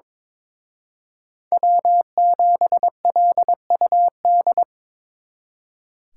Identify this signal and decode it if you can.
Morse code, heard as a keyed tone, about 25 wpm, 700 Hz. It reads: W7LUD
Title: Speed Racing - USA 1x3 Call Signs 22wpm